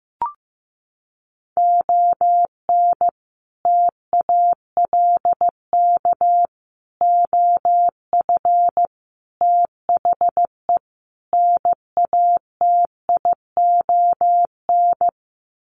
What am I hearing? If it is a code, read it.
Morse code, heard as a keyed tone, about 15 wpm, 700 Hz. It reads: ON TALK OF THE NATION
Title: Sets of 5 Words 15wpm